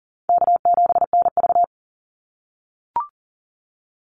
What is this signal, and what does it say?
Morse code, heard as a keyed tone, about 37 wpm, 700 Hz. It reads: K7NV
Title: Call Signs - 40wpm